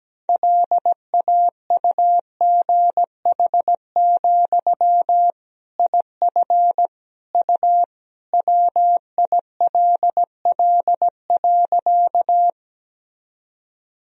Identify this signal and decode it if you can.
Morse code, heard as a keyed tone, about 17 wpm, 700 Hz. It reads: LAUGH, IF U WILL.
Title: Sentences from Top 500 Words and Abbreviations 17wpm